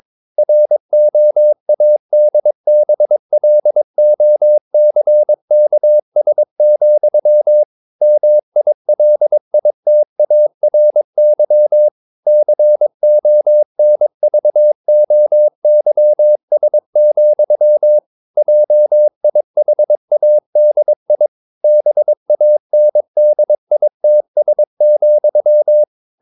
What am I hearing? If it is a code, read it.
Morse code, heard as a keyed tone, about 22 wpm, 600 Hz. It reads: ROADBLOCKS, MILITARY CONVOYS, JIHADI BANDITS,
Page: Romanen Never af Ken Follet på engelsk.